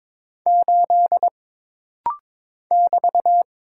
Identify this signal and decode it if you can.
Morse code, heard as a keyed tone, about 22 wpm, 700 Hz. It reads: 8 <BT>
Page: This rapid-fire timing is geared to push your ICR (Instant Character Recognition) skills.